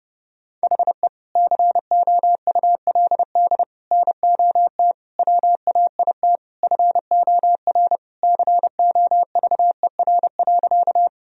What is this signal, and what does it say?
Morse code, heard as a keyed tone, about 32 wpm, 700 Hz. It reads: HE COULD NOT WAIT FOR COVER.
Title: Sentences from Top 1000 Words - Review Part 1 - 30wpm